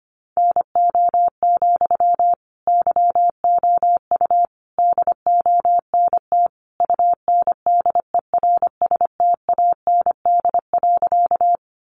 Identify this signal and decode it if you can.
Morse code, heard as a keyed tone, about 25 wpm, 700 Hz. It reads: NO, YOU DONT UNDERSTAND.
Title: Sentences from Top 600 Words 25wpm